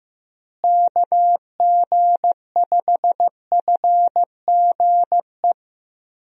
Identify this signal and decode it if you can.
Morse code, heard as a keyed tone, about 15 wpm, 700 Hz. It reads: KG5FGE
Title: USA Call Signs - 6 Characters - Repeated 2x - 15wpm